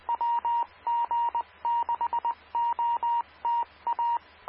Morse code, heard as a keyed tone, about 20 wpm, 940 Hz, with radio noise. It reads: WG6OTA
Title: Claremont, CA Police, Amateur Radio